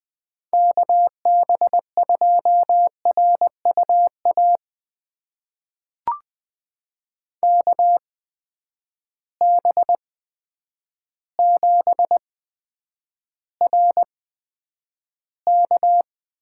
Morse code, heard as a keyed tone, about 20 wpm, spaced as for 9 wpm, 700 Hz. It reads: KB2RUA KB7RK
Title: USA Call Signs - 6 Characters - 8x Character Spacing - 20wpm